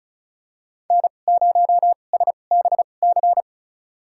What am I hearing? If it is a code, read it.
Morse code, heard as a keyed tone, about 34 wpm, spaced as for 25 wpm, 700 Hz. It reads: N0SBC
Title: USA Call Signs - 5 Characters - 2x Character Spacing - 35wpm